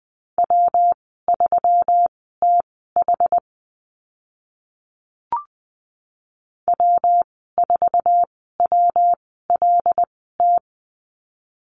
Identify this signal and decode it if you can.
Morse code, heard as a keyed tone, about 20 wpm, spaced as for 14 wpm, 700 Hz. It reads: W3TH W4WLT
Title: 140 Lesson 23g - Letter f - Call Signs 20wpm